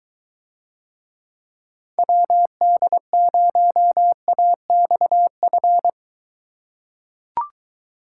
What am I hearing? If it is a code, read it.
Morse code, heard as a keyed tone, about 23 wpm, 700 Hz. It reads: WD0AXF